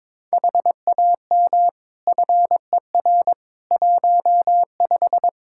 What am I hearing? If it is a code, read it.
Morse code, heard as a keyed tone, about 22 wpm, 700 Hz. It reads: HAM FER 15
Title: Practice QSO - Exchange 3 - More Info - 22wpm